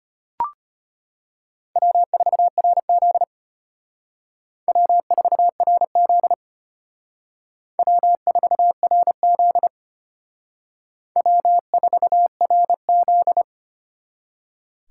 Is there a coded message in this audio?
Yes: Morse code from 1.8 to 13.4 s, heard as a keyed tone, about 32 wpm, 700 Hz. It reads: W4RZ W4RZ W4RZ W4RZ